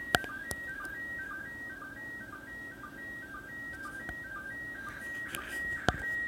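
SSTV (slow-scan television), PD 120.
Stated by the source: SSTV QSL card transmission from the International Space Station (ISS) at 2020 year-end, received on 145.800/146.800 MHz with a Yaesu FTM-3100.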